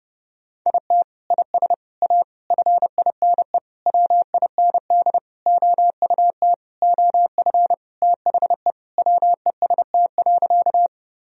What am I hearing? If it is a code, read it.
Morse code, heard as a keyed tone, about 32 wpm, 700 Hz. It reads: IT IS A FINE WIND OUT OF THE WEST.
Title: Sentences from Top 1000 Words - Review Part 2 - 30wpm